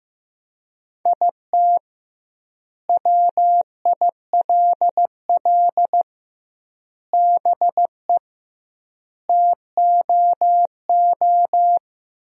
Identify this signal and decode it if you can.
Morse code, heard as a keyed tone, about 15 wpm, 700 Hz. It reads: IT WILL BE TOO
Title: 2x Word-Spacing - Sentences from Top 300 Words - 15wpm